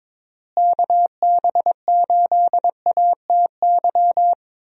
Morse code, heard as a keyed tone, about 22 wpm, 700 Hz. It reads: KB8ATY
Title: USA Call Signs - 6 Characters - Repeated 2x - 22wpm